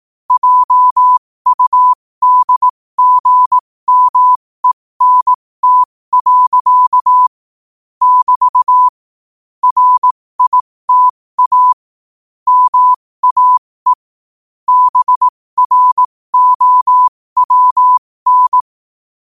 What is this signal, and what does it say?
Morse code, heard as a keyed tone, about 18 wpm, spaced as for 15 wpm, 1000 Hz. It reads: JUDGMENT. <BT> RITA MAE BROWN
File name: qotdfiles / 15wpm